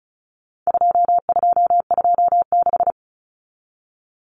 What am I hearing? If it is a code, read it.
Morse code, heard as a keyed tone, about 34 wpm, 700 Hz. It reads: 2226
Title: Common Cut Numbers - 4-Digit 35wpm